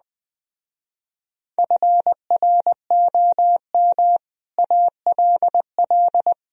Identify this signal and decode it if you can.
Morse code, heard as a keyed tone, about 20 wpm, 700 Hz. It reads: FROM ALL